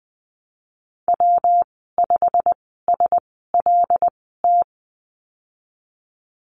Morse code, heard as a keyed tone, about 20 wpm, spaced as for 14 wpm, 700 Hz. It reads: W5SLT
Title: 154 Lesson 25g - Letter p - Call Signs 20wpm